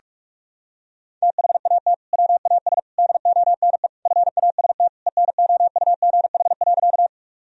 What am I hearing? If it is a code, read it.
Morse code, heard as a keyed tone, about 47 wpm, 700 Hz. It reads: THAT WAS DONE FAST ENOUGH.
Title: Sentences from Top 1000 Words - Review Part 2 - 45wpm